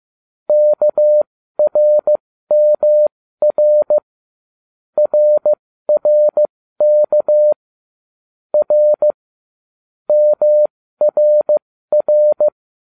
Morse code, heard as a keyed tone, about 15 wpm, spaced as for 13 wpm, 600 Hz. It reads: KRMR RRK R MRR